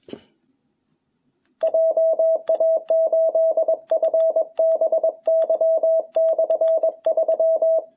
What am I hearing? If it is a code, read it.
Morse code, heard as a keyed tone, about 21 wpm, 660 Hz. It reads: JA8FBY/3
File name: jh8cbh / 2021cw